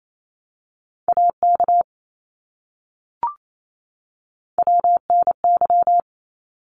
Morse code, heard as a keyed tone, about 28 wpm, 700 Hz. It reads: AK WNY
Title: ARRL Sections 28wpm